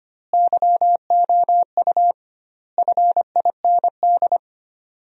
Morse code, heard as a keyed tone, about 25 wpm, 700 Hz. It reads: YOU FIND
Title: 2x Word-Spacing - Sentences from Top 200 Words - 25wpm